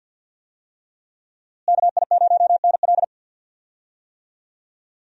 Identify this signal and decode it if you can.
Morse code, heard as a keyed tone, about 52 wpm, 700 Hz. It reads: KI0NL
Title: USA Call Signs - 5 Characters - Repeated 3x - 50wpm